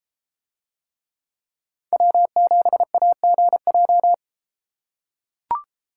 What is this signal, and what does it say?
Morse code, heard as a keyed tone, about 32 wpm, 700 Hz. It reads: W7AGJ